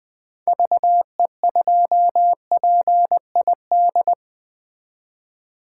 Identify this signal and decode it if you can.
Morse code, heard as a keyed tone, about 20 wpm, 700 Hz. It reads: VE2PID
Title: 2023 SOTA Chaser Call Signs - 20wpm